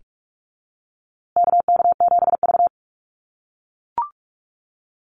Morse code, heard as a keyed tone, about 44 wpm, 700 Hz. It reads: KK7V
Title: USA Call Signs Repeated 2x 45wpm